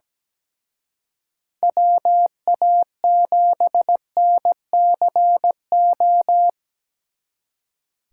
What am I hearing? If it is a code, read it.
Morse code, heard as a keyed tone, about 19 wpm, 700 Hz. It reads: WA7NCO
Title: Speed Racing - USA Call Signs 17wpm